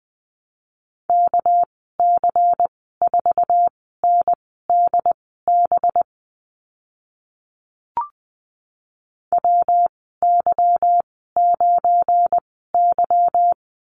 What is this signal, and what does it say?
Morse code, heard as a keyed tone, about 20 wpm, spaced as for 15 wpm, 700 Hz. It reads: KC4NDB WY9Y
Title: 231 Lesson 37g - Letter x - Call Signs 20wpm